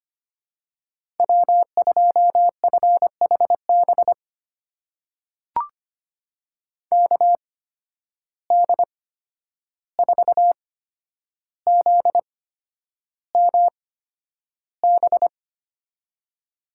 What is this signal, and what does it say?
Morse code, heard as a keyed tone, about 25 wpm, spaced as for 10 wpm, 700 Hz. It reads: W2FHB KD4ZMB
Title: USA Call Signs - 8x Character Spacing - 25wpm